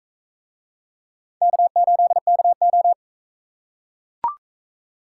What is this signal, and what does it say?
Morse code, heard as a keyed tone, about 41 wpm, 700 Hz. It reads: K8KO